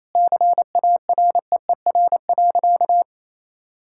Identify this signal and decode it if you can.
Morse code, heard as a keyed tone, about 28 wpm, 700 Hz. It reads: CAREER.
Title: Sentences from Top 1000 Words - Review Part 6 - 28wpm